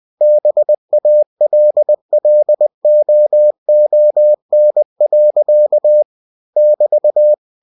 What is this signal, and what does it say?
Morse code, heard as a keyed tone, about 20 wpm, 600 Hz. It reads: BALLOON. <BT>
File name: cw-books / winnie-the-pooh / winnie-20wpm-page-0049